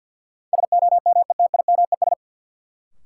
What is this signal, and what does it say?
Morse code, heard as a keyed tone, about 52 wpm, 700 Hz. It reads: SOMETIMES
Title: Sentences from Top 600 Words - New Word Drill - 50wpm